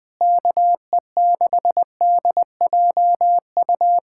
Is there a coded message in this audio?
Yes: Morse code from 0.2 to 4.0 s, heard as a keyed tone, about 20 wpm, 700 Hz. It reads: KE6DJU